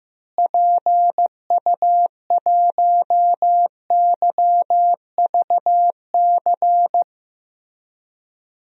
Morse code, heard as a keyed tone, about 15 wpm, 700 Hz. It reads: PU1YVC
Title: DXCC-100 Call Signs - 6 Characters - Repeated 3x - 15wpm